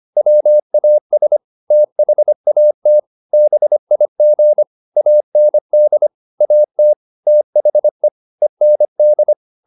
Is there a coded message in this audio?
Yes: Morse code from 0.2 to 9.3 s, heard as a keyed tone, about 25 wpm, 600 Hz. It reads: WAS THAT BIG AND AT THE END